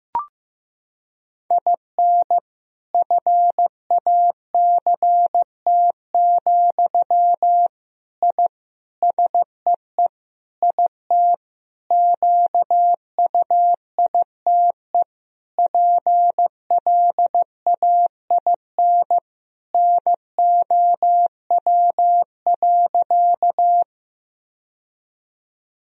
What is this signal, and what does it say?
Morse code, heard as a keyed tone, about 15 wpm, 700 Hz. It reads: IN FACT, I SEE IT QUITE PLAIN NOW.
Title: Sentences from Top 1000 Words - Review Part 3 - 15wpm